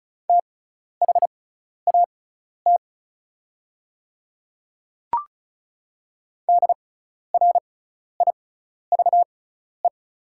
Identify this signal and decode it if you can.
Morse code, heard as a keyed tone, about 36 wpm, spaced as for 12 wpm, 700 Hz. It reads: THAT DRIVE
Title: 6x Character Spacing - Top 500 Words - 35wpm